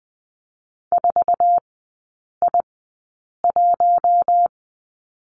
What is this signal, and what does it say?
Morse code, heard as a keyed tone, about 20 wpm, spaced as for 8 wpm, 700 Hz. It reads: 4I1
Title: 067 Lesson 12e - Letter d - Three Characters 20wpm